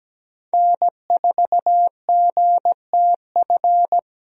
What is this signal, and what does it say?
Morse code, heard as a keyed tone, about 17 wpm, 700 Hz. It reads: N4GTF
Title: USA Call Signs - 5 Characters - Repeated 2x - 17wpm